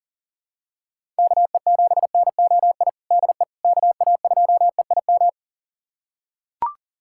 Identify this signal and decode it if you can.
Morse code, heard as a keyed tone, about 40 wpm, 700 Hz. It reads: KE7NOI DE KA2EIM
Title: USA Callsign DE Callsign - 6 Characters - 40wpm